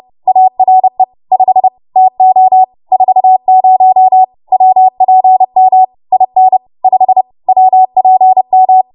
Morse code, heard as a keyed tone, about 30 wpm, 750 Hz. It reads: ARE 5 TO 40 WPM IN 5 WPM
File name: Morse / 30WPM